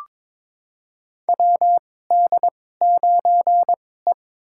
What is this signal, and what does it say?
Morse code, heard as a keyed tone, about 22 wpm, spaced as for 16 wpm, 700 Hz. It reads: WD9E
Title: USA Call Signs - 2x Character Spacing - 22wpm